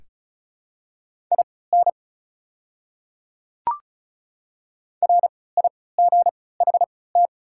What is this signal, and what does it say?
Morse code, heard as a keyed tone, about 34 wpm, spaced as for 20 wpm, 690 Hz. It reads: IN RIGHT
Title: 3x Character Spacing - Top 200 Words - 35wpm